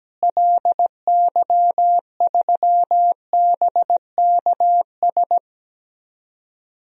Morse code, heard as a keyed tone, about 17 wpm, 700 Hz. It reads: LY3BKS
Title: DXCC-100 Call Signs - 6 Characters - Repeated 3x - 17wpm